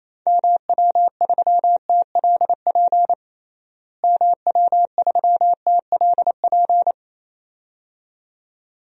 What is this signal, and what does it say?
Morse code, heard as a keyed tone, about 27 wpm, 700 Hz. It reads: MW3TLP MW3TLP
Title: DXCC-100 Call Signs - 6 Characters - Repeated 2x - 28wpm